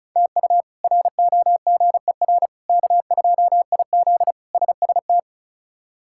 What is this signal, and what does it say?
Morse code, heard as a keyed tone, about 34 wpm, 700 Hz. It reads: TU ROGER K2IZ SST
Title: K1USN SST QSO Practice - 35wpm